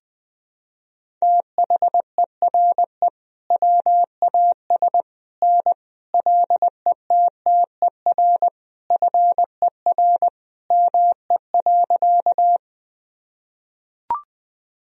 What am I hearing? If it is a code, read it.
Morse code, heard as a keyed tone, about 21 wpm, 700 Hz. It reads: THERE WAS N LETTER FER ME.
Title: Sentences from Top 300 Words and Abbreviations 20wpm